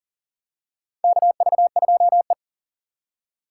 Morse code, heard as a keyed tone, about 40 wpm, 700 Hz. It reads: KV2E
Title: USA Call Signs - 3x Character Spacing - 40wpm